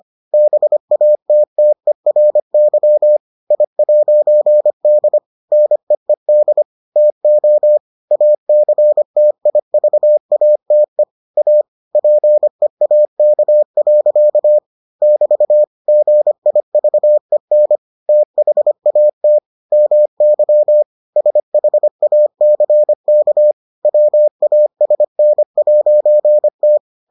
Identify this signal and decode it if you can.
Morse code, heard as a keyed tone, about 25 wpm, 600 Hz. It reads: BATTERY I'D NEED TO ACTIVATE A PEAK. <BT> GIVEN THAT MY SHACK WASN'T